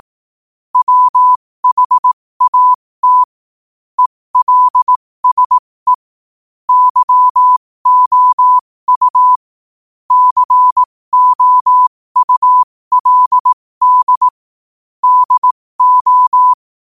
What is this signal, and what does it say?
Morse code, heard as a keyed tone, about 18 wpm, spaced as for 15 wpm, 1000 Hz. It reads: WHAT ELSE YOU COULD DO
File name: qotdfiles / 15wpm